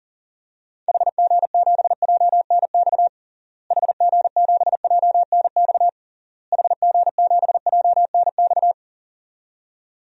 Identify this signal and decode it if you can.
Morse code, heard as a keyed tone, about 40 wpm, 700 Hz. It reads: HG7JNX HG7JNX HG7JNX
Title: DXCC-100 Call Signs - 6 Characters - Repeated 3x - 40wpm